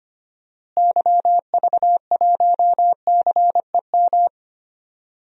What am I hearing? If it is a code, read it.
Morse code, heard as a keyed tone, about 25 wpm, 700 Hz. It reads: YV1CEM
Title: DXCC-100 Call Signs - 6 Characters - Repeated 3x - 25wpm